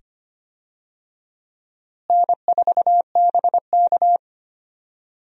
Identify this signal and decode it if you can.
Morse code, heard as a keyed tone, about 25 wpm, 700 Hz. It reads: N4BK
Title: USA Call Signs - 4 Characters - Repeated 2x - 25wpm